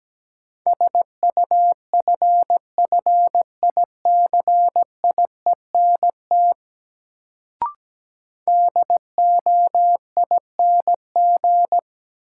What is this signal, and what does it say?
Morse code, heard as a keyed tone, about 17 wpm, 700 Hz. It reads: SUFFICIENT DOING
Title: Sentences from Top 1400 Words - New Word Drill Alt - 17wpm